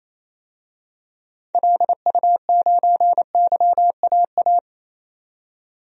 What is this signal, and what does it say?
Morse code, heard as a keyed tone, about 28 wpm, 700 Hz. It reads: LU9YAA
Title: DXCC-100 Call Signs - 6 Characters - 28wpm